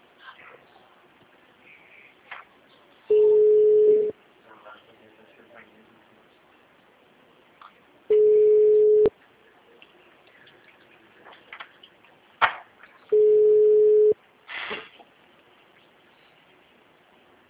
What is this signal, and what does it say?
Ringback tone (Europe, 425 Hz), with speech.